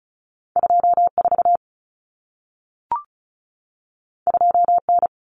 Morse code, heard as a keyed tone, about 36 wpm, 700 Hz. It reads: 24 2N
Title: Common Cut Numbers - 2-Digit 35wpm